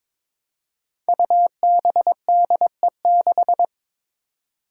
Morse code, heard as a keyed tone, about 22 wpm, 700 Hz. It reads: UBDE6
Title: Five Challenging Letter-Number Combinations 22wpm